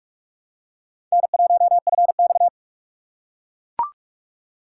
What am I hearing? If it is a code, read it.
Morse code, heard as a keyed tone, about 42 wpm, 700 Hz. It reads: N1FX
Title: USA Call Signs - 4 Characters - 2x Character Spacing - 45wpm